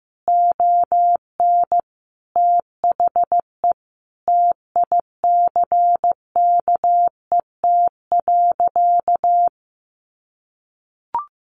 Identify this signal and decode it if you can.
Morse code, heard as a keyed tone, about 15 wpm, 700 Hz. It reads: ON THE TICKET.
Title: Sentences from Top 1400 Words 15wpm